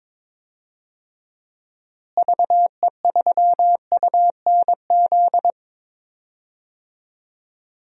Morse code, heard as a keyed tone, about 22 wpm, 700 Hz. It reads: VE3UNZ
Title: DXCC-100 Call Signs - 6 Characters - Repeated 2x - 22wpm